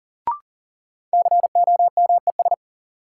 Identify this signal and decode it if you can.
Morse code, heard as a keyed tone, about 40 wpm, 700 Hz. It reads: COMES
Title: Sets of 3 Words - New Word Drill - 40wpm